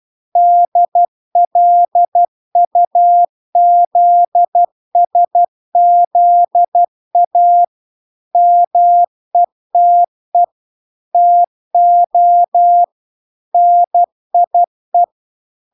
Morse code, heard as a keyed tone, about 12 wpm, 700 Hz. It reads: DLUZSZA METE TO NIE